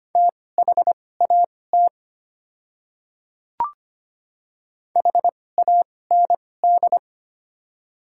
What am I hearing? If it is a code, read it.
Morse code, heard as a keyed tone, about 25 wpm, spaced as for 18 wpm, 700 Hz. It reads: THAT HAND
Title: 2x Character Spacing - Top 200 Words - 25wpm